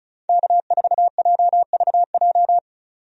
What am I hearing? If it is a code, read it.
Morse code, heard as a keyed tone, about 36 wpm, 700 Hz. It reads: K4JVJ
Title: USA Call Signs - 5 Characters - 35wpm